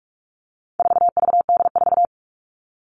Morse code, heard as a keyed tone, about 44 wpm, 700 Hz. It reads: 4VD4
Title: All Cut Numbers - 4-Digit 45wpm